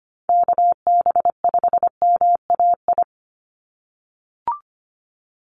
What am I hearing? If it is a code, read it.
Morse code, heard as a keyed tone, about 25 wpm, 700 Hz. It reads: KB5MAI
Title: USA Call Signs 25wpm